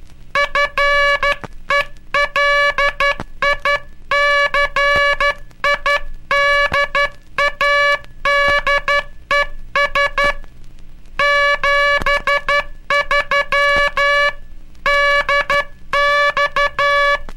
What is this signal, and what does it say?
Morse code, heard as a keyed tone, about 13 wpm, 1700 Hz, with noise. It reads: FELICIDADES 73 DX
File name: cursos / Curso CW mp3 / CW Aula - 18